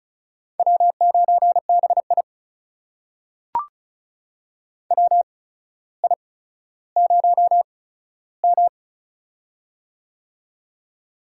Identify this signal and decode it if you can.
Morse code, heard as a keyed tone, about 33 wpm, spaced as for 15 wpm, 700 Hz. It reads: W9BI WI0M
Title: USA Call Signs - 4 Characters - 8x Character Spacing - 35wpm